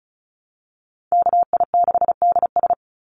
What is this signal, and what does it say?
Morse code, heard as a keyed tone, about 34 wpm, 700 Hz. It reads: KI6DS
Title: NA SOTA Chaser Callsigns 35wpm